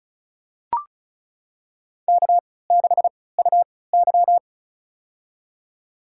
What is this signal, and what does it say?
Morse code, heard as a keyed tone, about 34 wpm, spaced as for 20 wpm, 700 Hz. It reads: K6UY
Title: USA Call Signs - 4 Characters - 3x Character Spacing - 35wpm